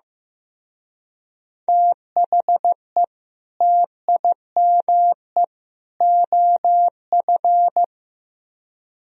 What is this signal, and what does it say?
Morse code, heard as a keyed tone, about 18 wpm, 700 Hz. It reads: THE TIME OF
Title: Speed-Racing - Sets of 3 Words 15wpm